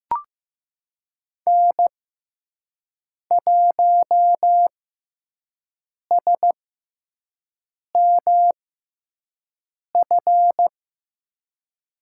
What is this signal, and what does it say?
Morse code, heard as a keyed tone, about 15 wpm, spaced as for 5 wpm, 700 Hz. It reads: N1SMF
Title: USA Call Signs - 5 Characters - 6x Character Spacing - 15wpm